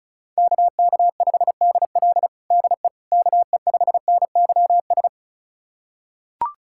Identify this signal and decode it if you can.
Morse code, heard as a keyed tone, about 36 wpm, 700 Hz. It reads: KK5DL DE KE5NYS
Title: USA Callsign DE Callsign 35wpm